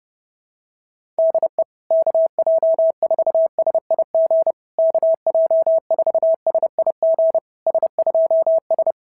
Morse code, heard as a keyed tone, about 28 wpm, 650 Hz. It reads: DE KJ4SIG KJ4SIG S2S
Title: SOTA QSO Practice 30wpm